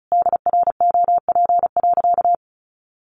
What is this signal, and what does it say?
Morse code, heard as a keyed tone, about 34 wpm, 700 Hz. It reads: DROP.
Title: Sentences from Top 1400 Words 35wpm